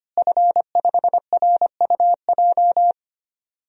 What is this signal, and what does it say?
Morse code, heard as a keyed tone, about 25 wpm, 700 Hz. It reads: F5RUJ
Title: DXCC-100 Call Signs - 5 Characters - Repeated 3x - 25wpm